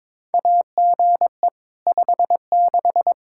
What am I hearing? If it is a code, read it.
Morse code, heard as a keyed tone, about 22 wpm, 700 Hz. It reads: AGE 56
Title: K1USN SST QSO Practice 22wpm